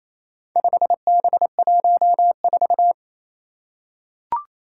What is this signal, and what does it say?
Morse code, heard as a keyed tone, about 27 wpm, 700 Hz. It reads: 5B14
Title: Four Challenging Letter-Number Combinations 28wpm